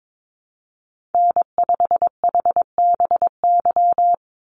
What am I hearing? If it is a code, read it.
Morse code, heard as a keyed tone, about 22 wpm, 700 Hz. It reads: N5HBY
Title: USA Call Signs 22wpm